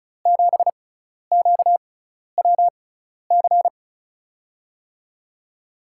Morse code, heard as a keyed tone, about 36 wpm, spaced as for 12 wpm, 700 Hz. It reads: 7QWC
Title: USA Call Signs - 6 Characters - 6x Character Spacing - 35wpm